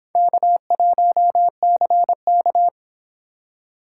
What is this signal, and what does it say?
Morse code, heard as a keyed tone, about 26 wpm, 700 Hz. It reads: K1CK